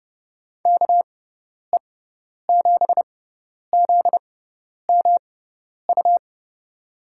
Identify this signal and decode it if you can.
Morse code, heard as a keyed tone, about 29 wpm, spaced as for 10 wpm, 700 Hz. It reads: KE7ZMU
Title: USA Call Signs - 6 Characters - 6x Character Spacing - 30wpm